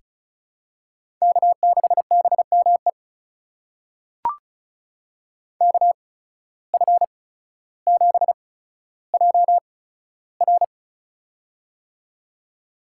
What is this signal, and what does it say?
Morse code, heard as a keyed tone, about 36 wpm, spaced as for 15 wpm, 700 Hz. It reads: K6BME KF7JR
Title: USA Call Signs - 8x Character Spacing - 35wpm